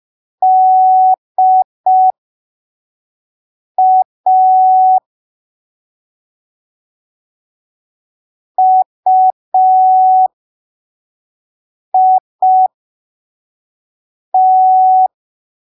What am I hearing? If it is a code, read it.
Morse code, heard as a keyed tone, about 5 wpm, spaced as for 3 wpm, 750 Hz. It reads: DA UIT
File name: arquivos / texto / texto 12 05ppm